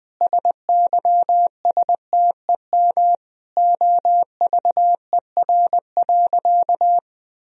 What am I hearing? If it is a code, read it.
Morse code, heard as a keyed tone, about 21 wpm, 700 Hz. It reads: SYSTEM OVER.
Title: Sentences from Top 1000 Words - Review Part 3 - 20wpm